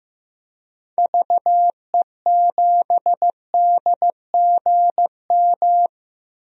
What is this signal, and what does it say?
Morse code, heard as a keyed tone, about 15 wpm, 700 Hz. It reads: VE7DGM
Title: DXCC-100 Call Signs - 6 Characters - Repeated 2x - 15wpm